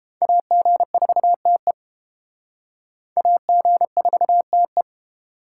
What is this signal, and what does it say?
Morse code, heard as a keyed tone, about 28 wpm, 700 Hz. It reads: AG4TE AG4TE